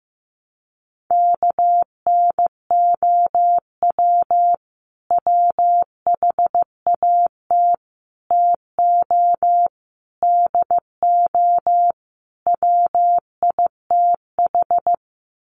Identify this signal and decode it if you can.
Morse code, heard as a keyed tone, about 15 wpm, 700 Hz. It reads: KNOW WHAT TO DO WITH
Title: Sets of 5 Words 15wpm